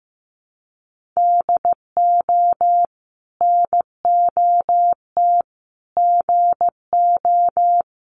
Morse code, heard as a keyed tone, about 15 wpm, 700 Hz. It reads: DO NOT GO
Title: Sets of 4 Words 15wpm